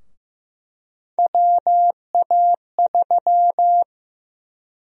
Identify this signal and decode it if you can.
Morse code, heard as a keyed tone, about 15 wpm, 700 Hz. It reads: WA3
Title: USA Call Sign Prefixes - Repeated 2x - 15wpm